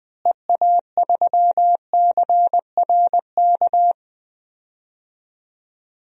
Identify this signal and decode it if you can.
Morse code, heard as a keyed tone, about 21 wpm, 700 Hz. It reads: EA3CRK
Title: DXCC-100 Call Signs - 6 Characters - Repeated 2x - 20wpm